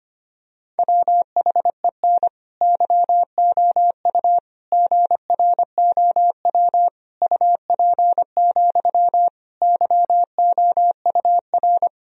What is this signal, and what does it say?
Morse code, heard as a keyed tone, about 25 wpm, 700 Hz. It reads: WHEN YOU GROW UP, YOUR
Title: Sentences from Top 1000 Words - Review Part 1 - 25wpm